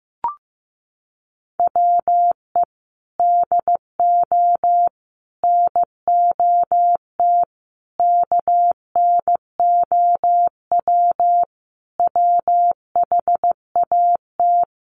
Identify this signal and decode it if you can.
Morse code, heard as a keyed tone, about 14 wpm, 700 Hz. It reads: WE DO NOT KNOW WHAT
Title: Sentences from Top 1500 Words 15wpm